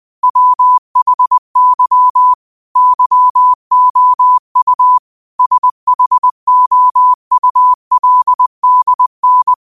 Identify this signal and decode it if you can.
Morse code, heard as a keyed tone, about 20 wpm, 1000 Hz. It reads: WHY YOU SHOULDN